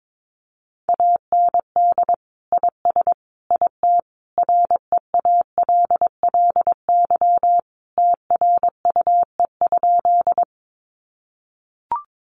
Morse code, heard as a keyed tone, about 22 wpm, 700 Hz. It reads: AND IS IT REALLY TRUE?
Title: Sentences from Top 600 Words 22wpm